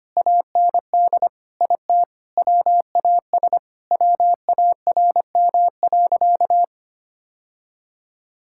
Morse code, heard as a keyed tone, about 25 wpm, 700 Hz. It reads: AND IT WAS WARM.
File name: practice / 25wpm / Sentences with 2-Word Phrases - 25wpm